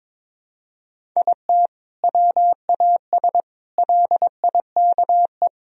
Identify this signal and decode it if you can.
Morse code, heard as a keyed tone, about 22 wpm, 700 Hz. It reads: IT WAS LIKE
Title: Sentences from Top 1000 Words - Review Part 1 - 22wpm